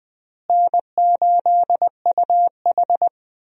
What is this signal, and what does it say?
Morse code, heard as a keyed tone, about 20 wpm, 700 Hz. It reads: N8UH
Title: USA Call Signs - 4 Characters - Repeated 2x - 20wpm